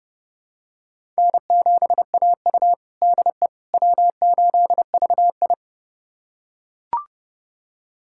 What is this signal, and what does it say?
Morse code, heard as a keyed tone, about 30 wpm, 700 Hz. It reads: N7AU DE W8VI
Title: USA Callsign DE Callsign - 4 Characters - 30wpm